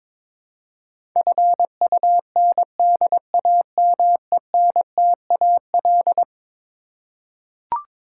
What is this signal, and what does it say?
Morse code, heard as a keyed tone, about 22 wpm, 700 Hz. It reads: FUNDAMENTAL
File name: practice / 22wpm / Eleven Letter Words - 22wpm